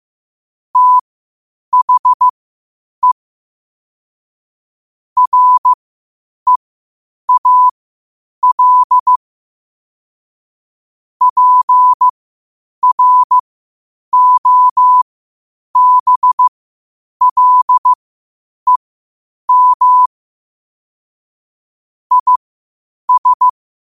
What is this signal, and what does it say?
Morse code, heard as a keyed tone, about 15 wpm, spaced as for 8 wpm, 1000 Hz. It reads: THE REAL PROBLEM IS
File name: qotdfiles / 8wpm